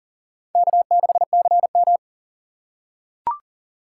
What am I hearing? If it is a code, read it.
Morse code, heard as a keyed tone, about 40 wpm, 700 Hz. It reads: K6CM
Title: USA Call Signs - 4 Characters - 2x Character Spacing - 40wpm